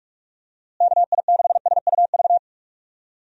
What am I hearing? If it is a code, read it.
Morse code, heard as a keyed tone, about 44 wpm, 700 Hz. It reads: KI6SUV
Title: USA Call Signs - 6 Characters - Repeated 2x - 45wpm